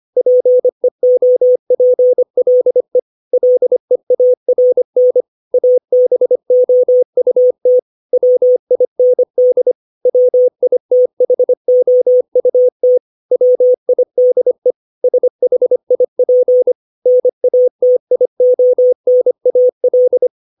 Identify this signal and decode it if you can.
Morse code, heard as a keyed tone, about 25 wpm, 500 Hz. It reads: PEOPLE LEARN ABOUT WIND WITHOUT WIDE SHIP NATIONAL
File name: 632 words / 632 Words 25WPM